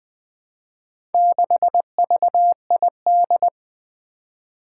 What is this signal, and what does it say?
Morse code, heard as a keyed tone, about 20 wpm, 700 Hz. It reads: 6VID
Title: Four Challenging Letter-Number Combinations 20wpm